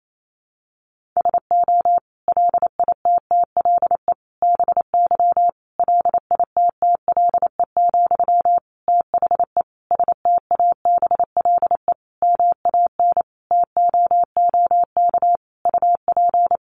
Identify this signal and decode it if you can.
Morse code, heard as a keyed tone, about 28 wpm, 700 Hz. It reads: SO LITTLE BY LITTLE, THE STABLE MAN TOOK UP
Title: Sentences from Top 1500 Words 28wpm